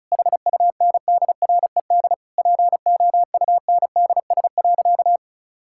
Morse code, heard as a keyed tone, about 36 wpm, 700 Hz. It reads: HUNDRED POUNDS.
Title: Sentences from Top 1000 Words - Review Part 6 - 35wpm